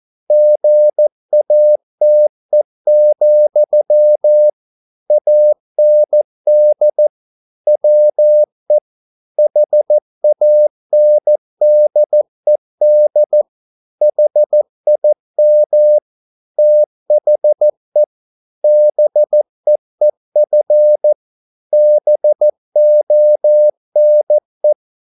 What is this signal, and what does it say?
Morse code, heard as a keyed tone, about 14 wpm, 600 Hz. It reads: GATE, AND WE HANDED HIM THE BEEF BONE
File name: mp3 files / a dogs mistake - 14wpm